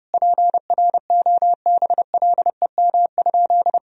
Morse code, heard as a keyed tone, about 30 wpm, 700 Hz. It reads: PROBLEM?
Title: Sentences from Top 1000 Words - Review Part 3 - 30wpm